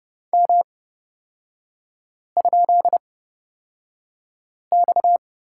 Morse code, heard as a keyed tone, about 30 wpm, spaced as for 5 wpm, 690 Hz. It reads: M?X
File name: Lesson 37e - Letter x - Three Characters 30wpm